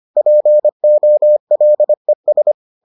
Morse code, heard as a keyed tone, about 25 wpm, 600 Hz. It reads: POLES